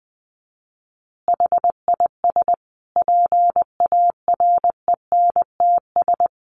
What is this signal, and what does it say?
Morse code, heard as a keyed tone, about 20 wpm, 700 Hz. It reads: HIS PARENTS
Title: Easy Phrases 20wpm